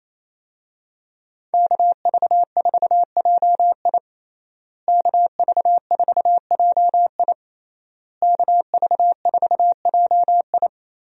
Morse code, heard as a keyed tone, about 28 wpm, 700 Hz. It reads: KV4JI KV4JI KV4JI
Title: USA Call Signs - 5 Characters - Repeated 3x - 28wpm